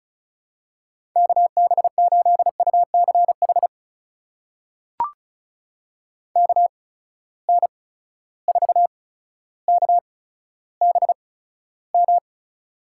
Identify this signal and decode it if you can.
Morse code, heard as a keyed tone, about 36 wpm, spaced as for 15 wpm, 700 Hz. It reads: KB8UCH KN4KBM
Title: USA Call Signs - 6 Characters - 8x Character Spacing - 35wpm